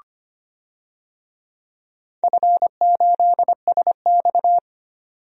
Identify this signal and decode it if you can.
Morse code, heard as a keyed tone, about 25 wpm, 700 Hz. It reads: F8SX
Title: DXCC-100 Call Signs - 4 Characters - 25wpm